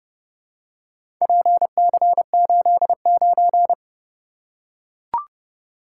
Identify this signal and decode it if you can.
Morse code, heard as a keyed tone, about 30 wpm, 700 Hz. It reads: PC89